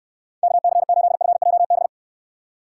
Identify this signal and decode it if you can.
Morse code, heard as a keyed tone, about 67 wpm, 700 Hz. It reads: KC8FJB